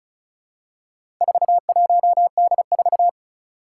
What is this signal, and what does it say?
Morse code, heard as a keyed tone, about 36 wpm, 700 Hz. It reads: 41D4
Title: Four Challenging Letter-Number Combinations 35wpm